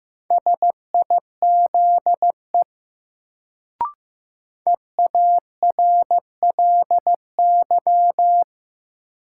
Morse code, heard as a keyed tone, about 15 wpm, 700 Hz. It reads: SIZE EARLY
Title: Sets of 3 Words - New Word Drill - 15wpm